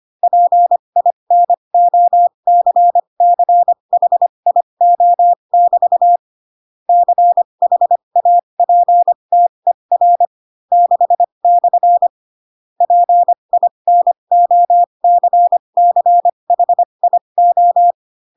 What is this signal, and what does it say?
Morse code, heard as a keyed tone, about 25 wpm, spaced as for 22 wpm, 700 Hz. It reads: PINOCCHIO<BT> CHAPTER 6/ PINOCCHIO
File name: Pinocchio / Pinocchio ch6 18 25 ik1hge